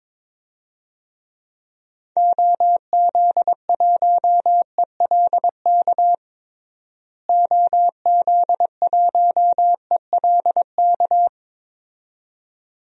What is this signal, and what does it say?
Morse code, heard as a keyed tone, about 22 wpm, 700 Hz. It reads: OZ1ELK OZ1ELK
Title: DXCC-100 Call Signs - 6 Characters - Repeated 2x - 22wpm